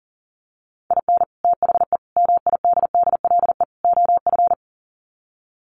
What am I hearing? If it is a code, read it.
Morse code, heard as a keyed tone, about 40 wpm, 700 Hz. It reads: IN THE MIDDLE OF
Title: Sets of 4 Words 40wpm